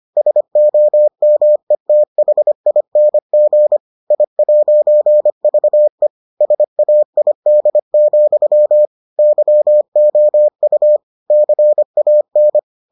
Morse code, heard as a keyed tone, about 25 wpm, 600 Hz. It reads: SOMETHING I'VE SAID, YOU CAN